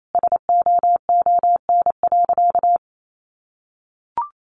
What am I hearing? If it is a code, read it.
Morse code, heard as a keyed tone, about 28 wpm, 700 Hz. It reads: SOON.
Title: Sentences from Top 1500 Words 28wpm